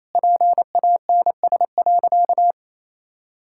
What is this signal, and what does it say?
Morse code, heard as a keyed tone, about 28 wpm, 700 Hz. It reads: PANS.
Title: Pangram Challenge v2 - 28wpm